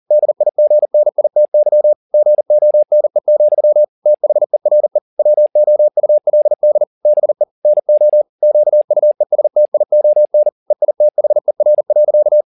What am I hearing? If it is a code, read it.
Morse code, heard as a keyed tone, about 40 wpm, 600 Hz. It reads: DIGNITY GONE, THERE WOULD BE NO QUESTION EITHER.
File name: the old man and the sea 2 40wpm